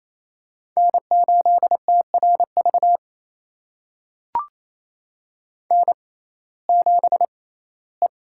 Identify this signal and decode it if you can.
Morse code, heard as a keyed tone, about 28 wpm, spaced as for 17 wpm, 700 Hz. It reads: N8TRV N7E
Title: USA Call Signs - 5 Characters - 6x Character Spacing - 28wpm